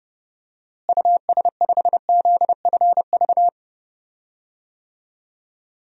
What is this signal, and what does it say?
Morse code, heard as a keyed tone, about 32 wpm, 700 Hz. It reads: US5ZFV
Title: DXCC-100 Call Signs - 6 Characters - Repeated 3x - 30wpm